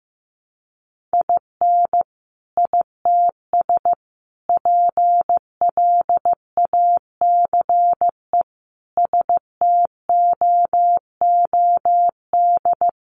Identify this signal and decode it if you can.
Morse code, heard as a keyed tone, about 15 wpm, 700 Hz. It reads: IN ITS PLACE STOOD
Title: Sentences from Top 1400 Words 15wpm